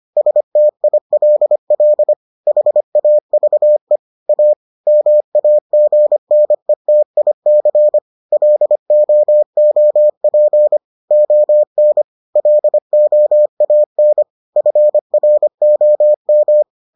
Morse code, heard as a keyed tone, about 25 wpm, 600 Hz. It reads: STILL HAVE A MAGNETIC LOOP ON LOAN FROM